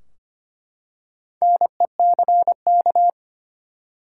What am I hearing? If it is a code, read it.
Morse code, heard as a keyed tone, about 25 wpm, 700 Hz. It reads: NECK
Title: Four Letter Words - 25wpm